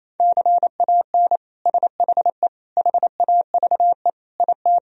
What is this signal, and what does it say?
Morse code, heard as a keyed tone, about 28 wpm, 700 Hz. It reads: CAN SHE HAVE IT
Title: Sentences from Top 1000 Words - Review Part 4 - 28wpm